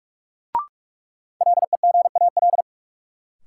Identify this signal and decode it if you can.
Morse code, heard as a keyed tone, about 42 wpm, 700 Hz. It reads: LEGAL
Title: Sentences from Top 1200 Words - New Word Drill - 45wpm